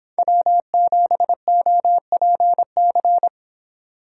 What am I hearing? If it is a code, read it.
Morse code, heard as a keyed tone, about 26 wpm, 700 Hz. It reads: W7OPC
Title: Speed Racing - USA 1x3 Call Signs 22wpm